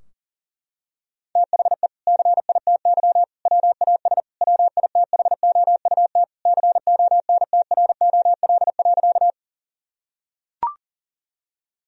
Morse code, heard as a keyed tone, about 37 wpm, 700 Hz. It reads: THE CITY WAS WITHOUT CONTROL.
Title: Sentences from Top 1000 Words - Review Part 3 - 40wpm